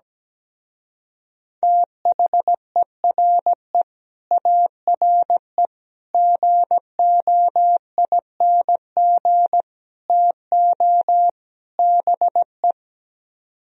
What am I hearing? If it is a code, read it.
Morse code, heard as a keyed tone, about 17 wpm, 700 Hz. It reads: THERE ARE GOING TO BE
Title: Speed-Racing - Sets of 5 Words 17wpm